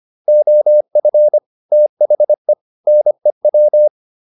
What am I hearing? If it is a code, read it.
Morse code, heard as a keyed tone, about 25 wpm, 600 Hz. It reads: OF THE NEW